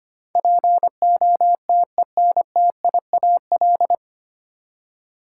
Morse code, heard as a keyed tone, about 25 wpm, 700 Hz. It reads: POTENTIAL
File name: Nine Letter Words - 25wpm